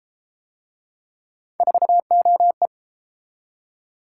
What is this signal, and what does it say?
Morse code, heard as a keyed tone, about 31 wpm, 700 Hz. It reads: 4OE